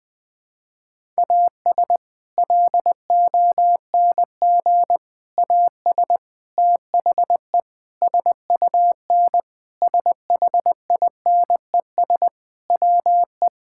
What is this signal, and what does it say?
Morse code, heard as a keyed tone, about 20 wpm, 700 Hz. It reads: AS LONG AS THE SUN SHINES WE
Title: Sentences with 4-Word Phrases - 20wpm